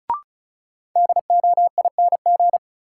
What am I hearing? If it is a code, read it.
Morse code, heard as a keyed tone, about 36 wpm, 700 Hz. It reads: DOING
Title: Sets of 4 Words - New Word Drill - 35wpm